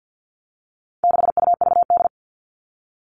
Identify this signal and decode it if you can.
Morse code, heard as a keyed tone, about 50 wpm, 700 Hz. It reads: 6UVD
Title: All Cut Numbers - 4-Digit 50wpm